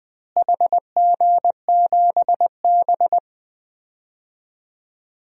Morse code, heard as a keyed tone, about 21 wpm, 700 Hz. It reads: HG7B
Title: DXCC-100 Call Signs - 4 Characters - Repeated 2x - 20wpm